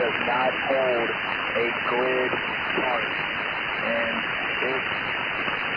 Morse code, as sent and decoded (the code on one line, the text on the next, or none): none